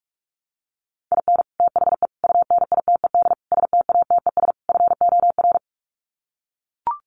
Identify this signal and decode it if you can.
Morse code, heard as a keyed tone, about 44 wpm, 700 Hz. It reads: IN THE UNITED STATES FOR
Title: Sets of 5 Words Encore2 45wpm